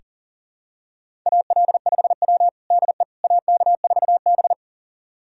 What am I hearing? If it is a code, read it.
Morse code, heard as a keyed tone, about 40 wpm, 700 Hz. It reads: AL5W DE AK4B
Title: USA Callsign DE Callsign - 4 Characters - 40wpm